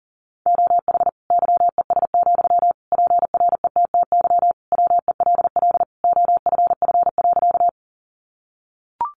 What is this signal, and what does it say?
Morse code, heard as a keyed tone, about 40 wpm, 700 Hz. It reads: OH YES, PRETTY WELL OFF.
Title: Sentences from Top 1000 Words 40wpm